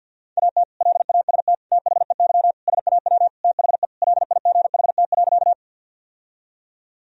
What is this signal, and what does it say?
Morse code, heard as a keyed tone, about 49 wpm, 700 Hz. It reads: AT LAST THEY SAW THE LIGHT.
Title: Sentences from Top 1000 Words - Review Part 1 - 50wpm